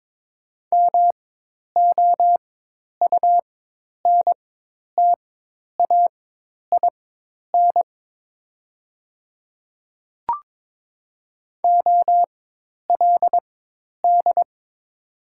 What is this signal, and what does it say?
Morse code, heard as a keyed tone, about 22 wpm, spaced as for 10 wpm, 700 Hz. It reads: MOUNTAIN OLD
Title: 4x Character Spacing - Top 300 Words - 22wpm